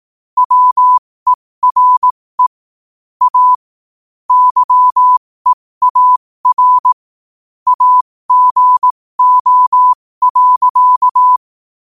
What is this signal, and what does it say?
Morse code, heard as a keyed tone, about 18 wpm, spaced as for 15 wpm, 1000 Hz. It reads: WERE A YEAR AGO.
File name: qotdfiles / 15wpm